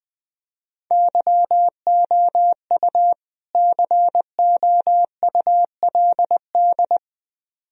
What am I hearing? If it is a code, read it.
Morse code, heard as a keyed tone, about 20 wpm, 700 Hz. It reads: YOU COULD
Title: Speed-Racing - Sets of 2 Words 17wpm